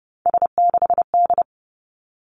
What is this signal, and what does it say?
Morse code, heard as a keyed tone, about 30 wpm, 700 Hz. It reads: S6D
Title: Three Challenging Letter-Number Combinations 30wpm